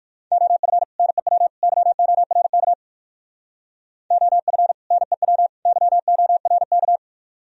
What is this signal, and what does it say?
Morse code, heard as a keyed tone, about 57 wpm, 700 Hz. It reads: OF NEW YORK OF NEW YORK